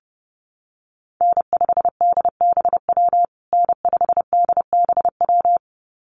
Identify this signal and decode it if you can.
Morse code, heard as a keyed tone, about 30 wpm, 700 Hz. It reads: N5DBW N5DBW
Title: USA Call Signs Repeated 2x 30wpm